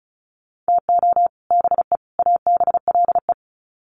Morse code, heard as a keyed tone, about 34 wpm, 700 Hz. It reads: TO BE ABLE
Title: Sets of 3 Words 35wpm